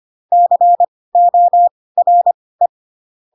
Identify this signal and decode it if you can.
Morse code, heard as a keyed tone, about 25 wpm, spaced as for 18 wpm, 700 Hz. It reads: CORE
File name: Pinocchio / Pinocchio ch7 12 25 ik1hge